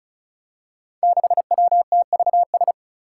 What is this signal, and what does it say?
Morse code, heard as a keyed tone, about 34 wpm, 700 Hz. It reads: 6WTVS
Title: Five Challenging Letter-Number Combinations 35wpm